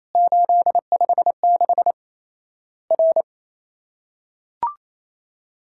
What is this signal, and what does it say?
Morse code, heard as a keyed tone, about 28 wpm, 700 Hz. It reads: 856 R
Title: Worked All Europe DX Contest - QTC Exchange - 28wpm